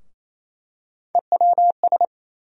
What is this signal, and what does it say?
Morse code, heard as a keyed tone, about 28 wpm, 700 Hz. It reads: EWS
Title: USA Call Sign Postfixes - 3 Letters 28wpm